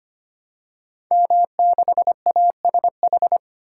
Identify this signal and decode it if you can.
Morse code, heard as a keyed tone, about 25 wpm, 700 Hz. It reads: M6ASH
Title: Code Groups - 5 Letter:Numbers - 25wpm